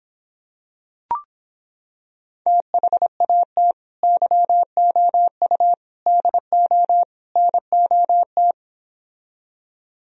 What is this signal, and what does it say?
Morse code, heard as a keyed tone, about 26 wpm, 700 Hz. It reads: THAT YOU DO NOT